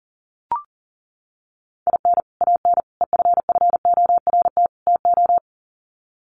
Morse code, heard as a keyed tone, about 40 wpm, 700 Hz. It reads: IN AN EFFORT TO
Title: Sets of 4 Words 40wpm